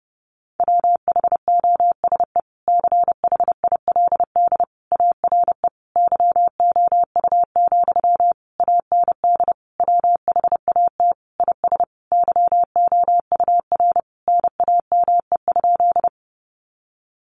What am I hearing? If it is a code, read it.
Morse code, heard as a keyed tone, about 30 wpm, 700 Hz. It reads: WHOSE CHILD ARE YOU, AND WHAT IS YOUR NAME?
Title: Sentences from Top 1200 Words 30wpm